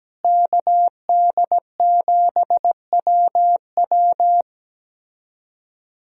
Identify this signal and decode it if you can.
Morse code, heard as a keyed tone, about 17 wpm, 700 Hz. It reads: KD7WW
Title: USA Callsign DE Callsign 17wpm